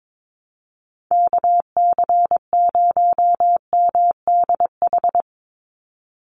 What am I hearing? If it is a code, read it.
Morse code, heard as a keyed tone, about 22 wpm, 700 Hz. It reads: KC0MDH
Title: USA Call Signs 22wpm